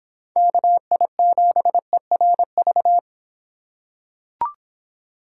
Morse code, heard as a keyed tone, about 26 wpm, 700 Hz. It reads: KI7ERV